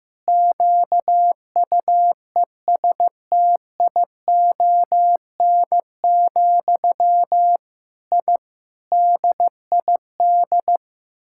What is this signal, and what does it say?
Morse code, heard as a keyed tone, about 15 wpm, 700 Hz. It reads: QUESTION, I DID
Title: Sentences from Top 1000 Words - Review Part 6 - 15wpm